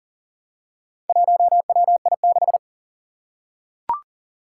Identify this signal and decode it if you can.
Morse code, heard as a keyed tone, about 40 wpm, 700 Hz. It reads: 1WI6
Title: Four Challenging Letter-Number Combinations 40wpm